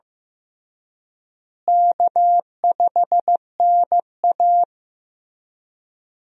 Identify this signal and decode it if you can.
Morse code, heard as a keyed tone, about 15 wpm, 700 Hz. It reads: K5NA
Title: USA Call Signs - 4 Characters - Repeated 3x - 15wpm